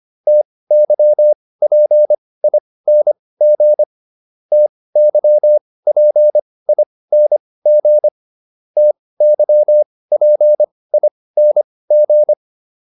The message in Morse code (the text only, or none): TYPING TYPING TYPING